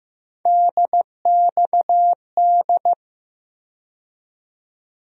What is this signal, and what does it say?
Morse code, heard as a keyed tone, about 15 wpm, 700 Hz. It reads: DXD
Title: USA Call Sign Postfixes - 3 Letters 15wpm